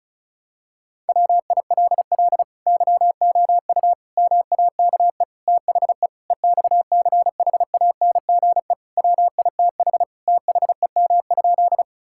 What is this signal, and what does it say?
Morse code, heard as a keyed tone, about 34 wpm, 700 Hz. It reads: WILL YOU MAKE THE EXCHANGE WITH THEM?
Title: Sentences from Top 1000 Words - Review Part 6 - 35wpm